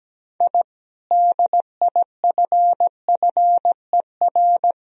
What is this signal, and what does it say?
Morse code, heard as a keyed tone, about 17 wpm, 700 Hz. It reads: I DIFFER
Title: Sentences from Top 1000 Words - Review Part 1 - 17wpm